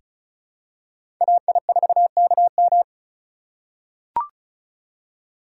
Morse code, heard as a keyed tone, about 34 wpm, 700 Hz. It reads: AI4KM
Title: USA Call Signs - 5 Characters - 2x Character Spacing - 35wpm